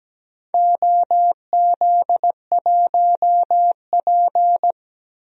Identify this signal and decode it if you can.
Morse code, heard as a keyed tone, about 17 wpm, 700 Hz. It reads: OZ1P
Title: DXCC-100 Call Signs - 4 Characters - 17wpm